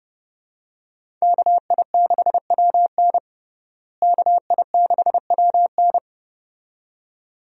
Morse code, heard as a keyed tone, about 32 wpm, 700 Hz. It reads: KI6WN KI6WN
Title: USA Call Signs - 5 Characters - Repeated 2x - 30wpm